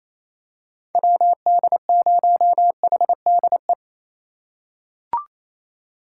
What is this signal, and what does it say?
Morse code, heard as a keyed tone, about 28 wpm, 700 Hz. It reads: WD0HDE
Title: USA Call Signs - 6 Characters - 4x Character Spacing - 28wpm